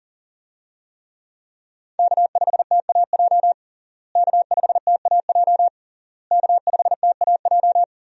Morse code, heard as a keyed tone, about 40 wpm, 700 Hz. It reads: K5TAJ K5TAJ K5TAJ
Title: USA Call Signs - 5 Characters - Repeated 3x - 40wpm